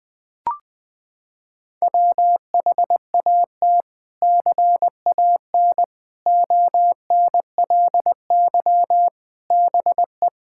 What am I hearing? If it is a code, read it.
Morse code, heard as a keyed tone, about 20 wpm, 700 Hz. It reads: WHAT CAN ONLY BE
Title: Sets of 6 Words - 20wpm